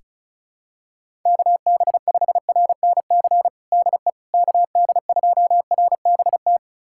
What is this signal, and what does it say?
Morse code, heard as a keyed tone, about 36 wpm, 700 Hz. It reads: KB5RNC DE KD2RBT
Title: USA Callsign DE Callsign - 6 Characters - 35wpm